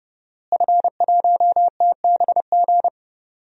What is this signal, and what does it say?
Morse code, heard as a keyed tone, about 30 wpm, 700 Hz. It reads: F1TBG
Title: DXCC-100 Call Signs - 5 Characters - Repeated 3x - 30wpm